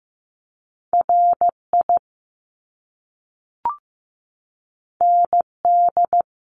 Morse code, heard as a keyed tone, about 15 wpm, 700 Hz. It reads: RI ND
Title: Top Two Letter Combinations in Words 15wpm